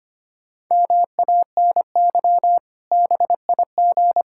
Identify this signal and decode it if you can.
Morse code, heard as a keyed tone, about 25 wpm, 700 Hz. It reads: MANY BIG
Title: Pangram Challenge v2 - 25wpm